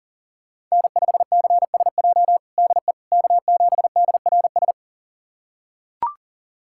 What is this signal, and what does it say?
Morse code, heard as a keyed tone, about 40 wpm, 700 Hz. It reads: N5CSJ DE K7DRS
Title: USA Callsign DE Callsign - 5 Characters - 40wpm